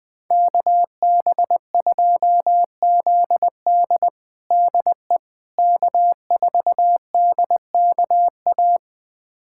Morse code, heard as a keyed tone, about 20 wpm, 700 Hz. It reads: KB2ZD DE K4DKA
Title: USA Callsign DE Callsign - 5 Characters - 20wpm